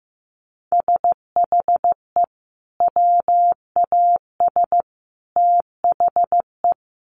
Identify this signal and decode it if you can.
Morse code, heard as a keyed tone, about 14 wpm, 700 Hz. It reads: SHE WAS THE
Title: Sentences from Top 200 Words 15wpm